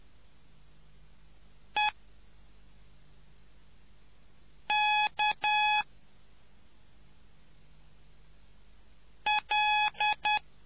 Morse code, heard as a keyed tone, about 10 wpm, 840 Hz. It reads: E K L